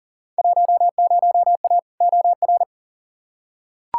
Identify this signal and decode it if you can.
Morse code, heard as a keyed tone, about 40 wpm, 700 Hz. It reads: 10A OR
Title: Field Day Exchanges - 40wpm